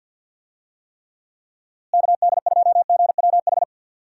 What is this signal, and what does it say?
Morse code, heard as a keyed tone, about 51 wpm, 700 Hz. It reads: KD2GWH
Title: USA Call Signs - 6 Characters - Repeated 3x - 50wpm